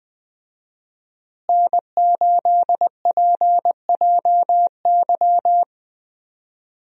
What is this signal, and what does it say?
Morse code, heard as a keyed tone, about 19 wpm, 700 Hz. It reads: N8PJY
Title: USA Call Signs - 5 Characters - Repeated 3x - 20wpm